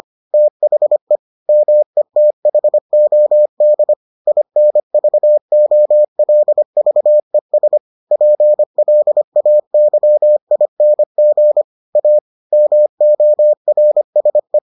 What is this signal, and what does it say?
Morse code, heard as a keyed tone, about 25 wpm, 600 Hz. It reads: THE METHOD INVOLVES PLAYING A MORSE